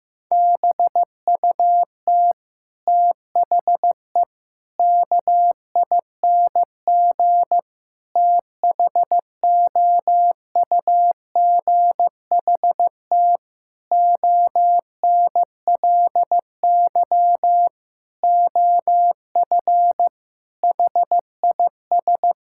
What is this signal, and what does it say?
Morse code, heard as a keyed tone, about 15 wpm, 700 Hz. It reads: BUT THE KING THOUGHT ONLY OF HIS
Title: Sentences from Top 1600 Words 15wpm